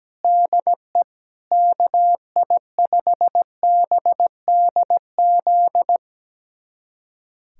Morse code, heard as a keyed tone, about 17 wpm, 700 Hz. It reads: DE KI5BDZ
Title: USA Callsign DE Callsign - 6 Characters - 17wpm